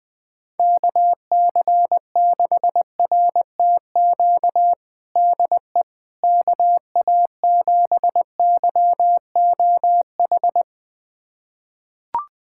Morse code, heard as a keyed tone, about 20 wpm, 700 Hz. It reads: KC6RTQ DE KA7YOH
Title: USA Callsign DE Callsign - 6 Characters - 20wpm